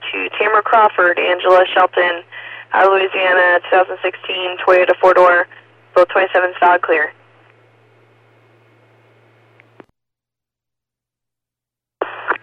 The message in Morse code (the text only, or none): none